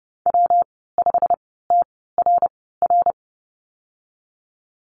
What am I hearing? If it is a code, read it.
Morse code, heard as a keyed tone, about 30 wpm, spaced as for 17 wpm, 700 Hz. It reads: W5TRR
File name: Lesson 19g - Letter w - Call Signs 30wpm